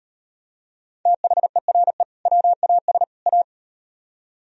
Morse code, heard as a keyed tone, about 38 wpm, 700 Hz. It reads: THERE WAS A